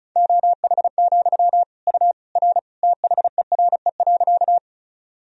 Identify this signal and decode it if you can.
Morse code, heard as a keyed tone, about 36 wpm, 700 Hz. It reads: OH, U R THERE.
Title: Sentences from Top 500 Words and Abbreviations 35wpm